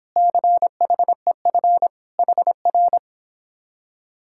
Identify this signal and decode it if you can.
Morse code, heard as a keyed tone, about 26 wpm, 700 Hz. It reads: CHEF HR